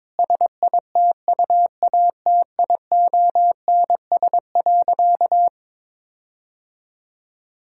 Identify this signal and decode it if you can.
Morse code, heard as a keyed tone, about 22 wpm, 700 Hz. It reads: SITUATIONS.
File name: Difficult Sentences with Hints - 22wpm